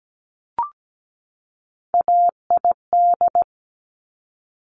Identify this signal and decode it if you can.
Morse code, heard as a keyed tone, about 17 wpm, 700 Hz. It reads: AID
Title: Top Three Letter Combinations in Words 17wpm